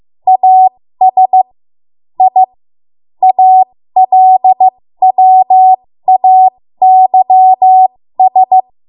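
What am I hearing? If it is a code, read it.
Morse code, heard as a keyed tone, about 15 wpm, spaced as for 13 wpm, 750 Hz. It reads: AS I ALWAYS
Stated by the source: ARRL Morse code practice at 13 WPM.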